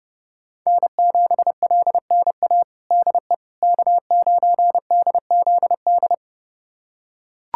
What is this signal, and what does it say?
Morse code, heard as a keyed tone, about 30 wpm, 700 Hz. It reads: N7LNA DE K9DZD
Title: USA Callsign DE Callsign - 5 Characters - 30wpm